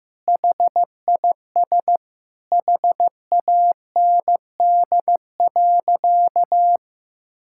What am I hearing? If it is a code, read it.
Morse code, heard as a keyed tone, about 15 wpm, 700 Hz. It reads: HIS HAND.
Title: Sentences from Top 1000 Words - Review Part 6 - 15wpm